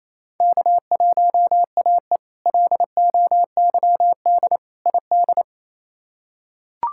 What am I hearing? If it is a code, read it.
Morse code, heard as a keyed tone, about 28 wpm, 700 Hz. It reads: K1AE LOYD ID
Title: K1USN SST QSO Practice - 28wpm